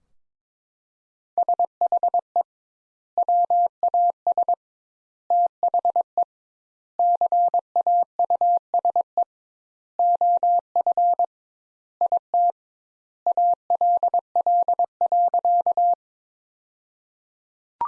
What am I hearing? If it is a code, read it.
Morse code, heard as a keyed tone, about 22 wpm, 700 Hz. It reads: SHE WAS THE CAUSE OF IT ALL.
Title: Sentences Top 200 Words - 22wpm